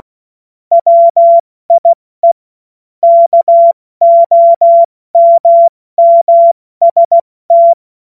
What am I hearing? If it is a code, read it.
Morse code, heard as a keyed tone, about 16 wpm, 680 Hz. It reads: WIE KOMMST
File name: Morsekurs / Witze / Witz 13. September 2022 70 F80